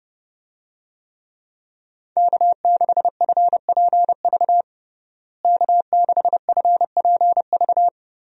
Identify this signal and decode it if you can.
Morse code, heard as a keyed tone, about 30 wpm, 700 Hz. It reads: K6FPV K6FPV
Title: USA Call Signs - 5 Characters - Repeated 3x - 30wpm